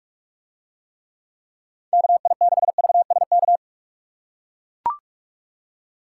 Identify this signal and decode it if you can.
Morse code, heard as a keyed tone, about 43 wpm, 700 Hz. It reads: KI6VSK